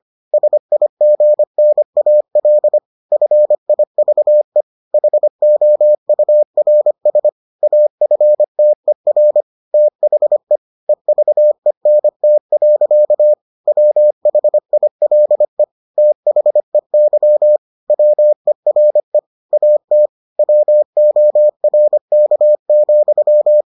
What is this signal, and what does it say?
Morse code, heard as a keyed tone, about 25 wpm, 600 Hz. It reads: SIGNAL FIVE HOURS AFTER THE EVENT. WHILE THEY WERE AT WORK,